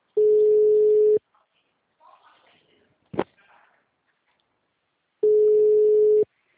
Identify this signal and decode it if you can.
Ringback tone (Europe, 425 Hz).